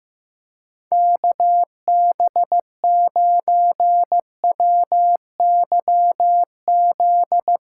Morse code, heard as a keyed tone, about 15 wpm, 700 Hz. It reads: KB9WYZ
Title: USA Call Signs - 6 Characters - 8x Character Spacing - 15wpm